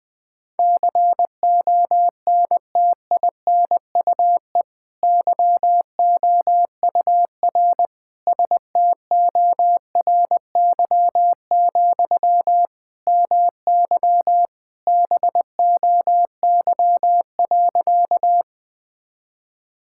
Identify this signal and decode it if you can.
Morse code, heard as a keyed tone, about 20 wpm, 700 Hz. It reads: CONTINUE YOUR STORY, MY BOY.
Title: Sentences from Top 1000 Words - Review Part 6 - 20wpm